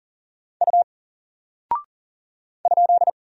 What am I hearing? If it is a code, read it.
Morse code, heard as a keyed tone, about 40 wpm, 700 Hz. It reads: U ?
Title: Single Character (Letter:Number:Punctuation:Prosign) 40wpm